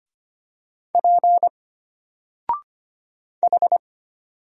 Morse code, heard as a keyed tone, about 25 wpm, 700 Hz. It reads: P H
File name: Single-Letter-Number-25wpm